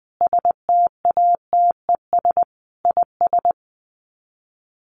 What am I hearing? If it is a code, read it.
Morse code, heard as a keyed tone, about 20 wpm, 700 Hz. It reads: STATES IS
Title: Sets of 5 Words Encore2 20wpm